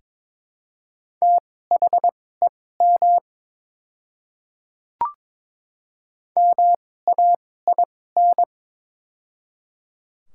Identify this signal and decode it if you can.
Morse code, heard as a keyed tone, about 22 wpm, spaced as for 16 wpm, 700 Hz. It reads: THEM MAIN
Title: 2x Character Spacing - Top 300 Words - 22wpm